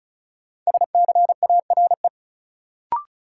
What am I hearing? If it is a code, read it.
Morse code, heard as a keyed tone, about 36 wpm, 700 Hz. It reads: SCARE
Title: Sentences from Top 1000 Words - Review Part 5 - New Word Drill - 35wpm